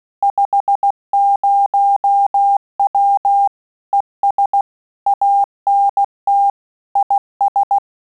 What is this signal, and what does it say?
Morse code, heard as a keyed tone, about 16 wpm, 800 Hz. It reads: 50W ES ANT IS
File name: QSO CW 75 CPM